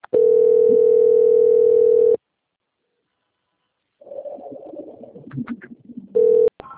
Ringback tone (North America), 440 + 480 Hz.
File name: out- - - - - .9265